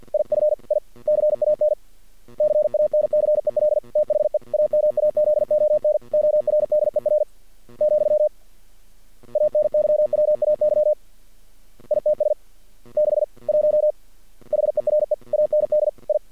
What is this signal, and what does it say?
Morse code, heard as a keyed tone, about 48 wpm, 600 Hz. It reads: IF I CAN CANCEL THE AMAZON ORDER , ANYWAY ITS 6 1 HERE AND I
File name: archived / CRSnet